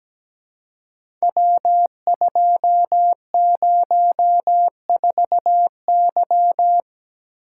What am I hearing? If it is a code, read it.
Morse code, heard as a keyed tone, about 17 wpm, 700 Hz. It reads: W204Y
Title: Code Groups - 5 Letter:Numbers - 17wpm